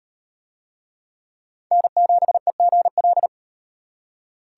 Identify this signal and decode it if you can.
Morse code, heard as a keyed tone, about 36 wpm, 700 Hz. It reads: N7EGL